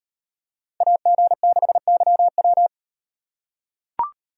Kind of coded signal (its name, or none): Morse code